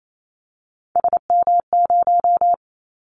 Morse code, heard as a keyed tone, about 28 wpm, 700 Hz. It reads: SM0
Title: Callsign Prefixes for Top 25 DXCC - 28wpm